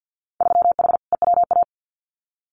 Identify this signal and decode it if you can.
Morse code, heard as a keyed tone, about 50 wpm, 700 Hz. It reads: 3H EWA
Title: Winter Field Day Exchanges Encore 50wpm